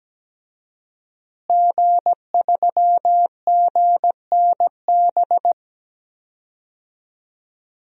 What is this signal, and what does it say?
Morse code, heard as a keyed tone, about 17 wpm, 700 Hz. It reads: G3GNB
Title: DXCC-100 Call Signs - 5 Characters - Repeated 3x - 17wpm